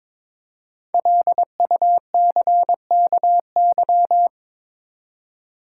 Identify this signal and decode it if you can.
Morse code, heard as a keyed tone, about 22 wpm, 700 Hz. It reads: LUCKY
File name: 5 Letter Words - First 3 Letters Given - 22wpm